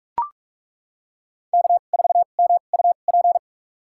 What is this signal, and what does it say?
Morse code, heard as a keyed tone, about 42 wpm, spaced as for 31 wpm, 700 Hz. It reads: K4MUP
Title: USA Call Signs - 2x Character Spacing - 45wpm